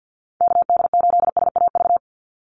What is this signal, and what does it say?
Morse code, heard as a keyed tone, about 50 wpm, 700 Hz. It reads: KD8SAV
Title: Call Signs - 50wpm